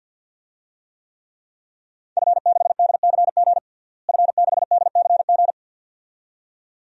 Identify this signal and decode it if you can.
Morse code, heard as a keyed tone, about 51 wpm, 700 Hz. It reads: F6DCG F6DCG
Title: DXCC-100 Call Signs - 5 Characters - Repeated 2x - 50wpm